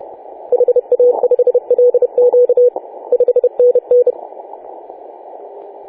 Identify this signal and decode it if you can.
Morse code, heard as a keyed tone, about 30 wpm, 490 Hz. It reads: HA5LQ 5NN
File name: 4W/G3ZEM 28.010MHz CW